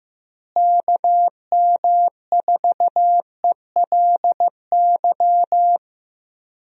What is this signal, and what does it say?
Morse code, heard as a keyed tone, about 14 wpm, 700 Hz. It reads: KM4ELY
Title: USA Call Signs - 6 Characters - Repeated 2x - 15wpm